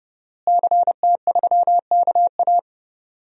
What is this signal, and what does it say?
Morse code, heard as a keyed tone, about 30 wpm, 700 Hz. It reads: CT3KA
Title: DXCC-100 Call Signs - 5 Characters - Repeated 3x - 30wpm